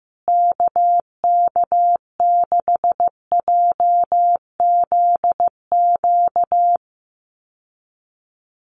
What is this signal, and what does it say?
Morse code, heard as a keyed tone, about 15 wpm, 700 Hz. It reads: KK6JZQ
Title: USA Call Signs Repeated 2x 15wpm